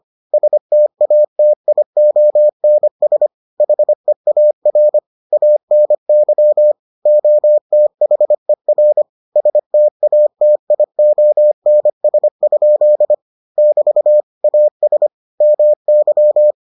Morse code, heard as a keyed tone, about 25 wpm, 600 Hz. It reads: STATIONS HEAR ANY OTHER STATIONS? <BT> AS MY